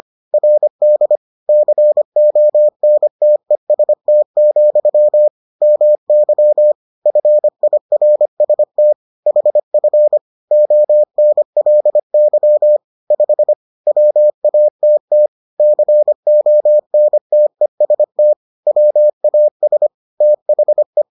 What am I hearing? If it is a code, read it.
Morse code, heard as a keyed tone, about 25 wpm, 600 Hz. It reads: RD CONTEST, MY FIRST HF ONLY 5 WATT CONTEST WAS THE